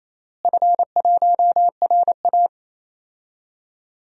Morse code, heard as a keyed tone, about 28 wpm, 700 Hz. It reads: F1RA
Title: DXCC-100 Call Signs - 4 Characters - Repeated 3x - 28wpm